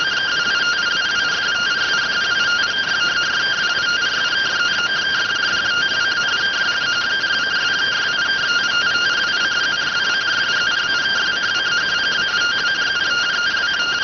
Two-tone teleprinter signal, about 170 Hz shift.